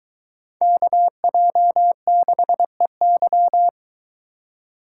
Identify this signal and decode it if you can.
Morse code, heard as a keyed tone, about 23 wpm, 700 Hz. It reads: KJ6EY